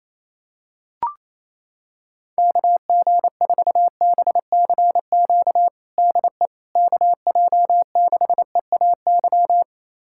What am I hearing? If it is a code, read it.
Morse code, heard as a keyed tone, about 28 wpm, 700 Hz. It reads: KG4BCQ DE KJ6EAY
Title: USA Callsign DE Callsign - 6 Characters - 28wpm